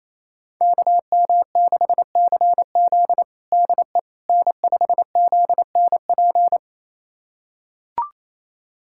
Morse code, heard as a keyed tone, about 28 wpm, 700 Hz. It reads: KM6CZ DE N5ZNP
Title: USA Callsign DE Callsign - 5 Characters - 28wpm